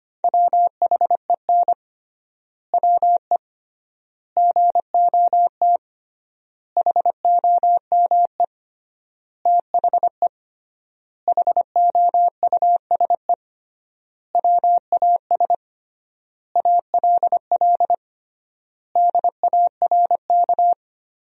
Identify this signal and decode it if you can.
Morse code, heard as a keyed tone, about 25 wpm, 700 Hz. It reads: WHEN WE GOT HOME THE HOUSE WAS ALL DARK
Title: 3x Word-Spacing - Sentences from Top 500 Words - 25wpm